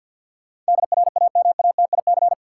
Morse code, heard as a keyed tone, about 53 wpm, 700 Hz. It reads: DRAMATIC
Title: Sentences from Top 1300 Words - New Word Drill - 50wpm